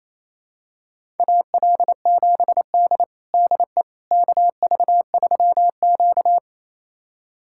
Morse code, heard as a keyed tone, about 28 wpm, 700 Hz. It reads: AL7D DE KV3Q
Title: USA Callsign DE Callsign - 4 Characters - 28wpm